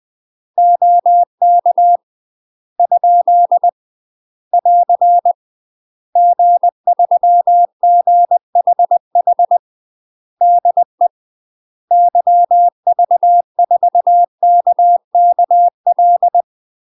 Morse code, heard as a keyed tone, about 20 wpm, 700 Hz. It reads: OK ? <AR> G3GHH DE YV4KKL
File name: UK MASTER QSOs TRAINING / 20WPM / UK MASTER QSOs 0012